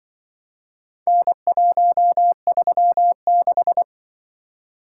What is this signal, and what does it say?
Morse code, heard as a keyed tone, about 25 wpm, 700 Hz. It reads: N136
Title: Speed-Racing - Four Letter-Number Combinations - 20wpm